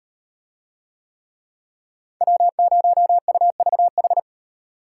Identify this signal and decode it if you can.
Morse code, heard as a keyed tone, about 38 wpm, 700 Hz. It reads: W0UVH